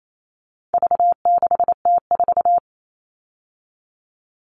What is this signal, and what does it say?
Morse code, heard as a keyed tone, about 28 wpm, 700 Hz. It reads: V6T4
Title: All Cut Numbers - 4-Digit 28wpm